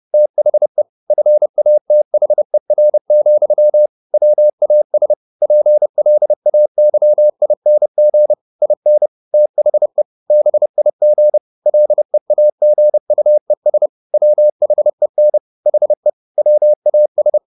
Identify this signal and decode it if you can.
Morse code, heard as a keyed tone, about 30 wpm, 600 Hz. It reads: THE FATHER, WAS PLAYING IN THE BIG LEAGUES WHEN HE WAS
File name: the old man and the sea / the old man and the sea 1 30wpm